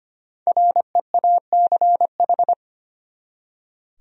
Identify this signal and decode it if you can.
Morse code, heard as a keyed tone, about 32 wpm, 700 Hz. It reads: REACH